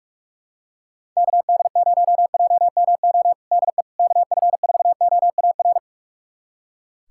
Morse code, heard as a keyed tone, about 46 wpm, 700 Hz. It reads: KD0JMO DE KF4OAR
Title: USA Callsign DE Callsign - 6 Characters - 45wpm